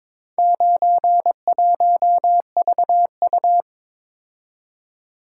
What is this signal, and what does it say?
Morse code, heard as a keyed tone, about 22 wpm, 700 Hz. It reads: 91VU
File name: practice / 22wpm / Four Letter-Number Combinations - 22wpm